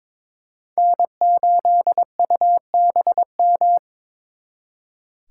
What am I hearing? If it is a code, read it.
Morse code, heard as a keyed tone, about 26 wpm, 700 Hz. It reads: N8UBM
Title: Speed Racing - USA 1x3 Call Signs 22wpm